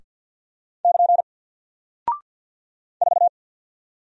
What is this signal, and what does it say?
Morse code, heard as a keyed tone, about 48 wpm, 700 Hz. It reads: <KN> 4
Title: Single Character (Letter:Number:Punctuation:Prosign) 50wpm